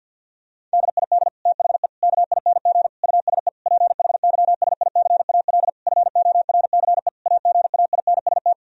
Morse code, heard as a keyed tone, about 50 wpm, 700 Hz. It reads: DID THE KING USE PHYSICAL FORCE AGAINST
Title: Sentences from Top 1000 Words - Review Part 4 - 50wpm